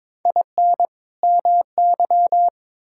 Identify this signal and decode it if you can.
Morse code, heard as a keyed tone, about 22 wpm, 700 Hz. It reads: IN MY
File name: Build A Phrase - 2-Word Phrases - 22wpm